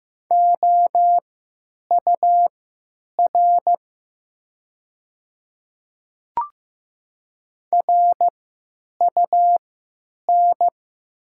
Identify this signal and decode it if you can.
Morse code, heard as a keyed tone, about 15 wpm, spaced as for 9 wpm, 700 Hz. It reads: OUR RUN
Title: 3x Character Spacing - Top 300 Words - 15wpm